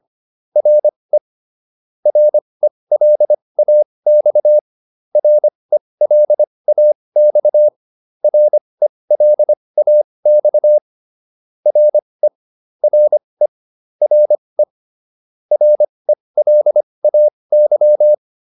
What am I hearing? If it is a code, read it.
Morse code, heard as a keyed tone, about 25 wpm, spaced as for 20 wpm, 600 Hz. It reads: RE RELAX RELAX RELAX RE RE RE RELAY